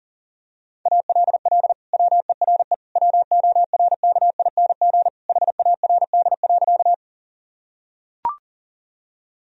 Morse code, heard as a keyed tone, about 40 wpm, 700 Hz. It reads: ALL WERE WORKING HARD.
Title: Sentences from Top 1000 Words - Review Part 4 - 40wpm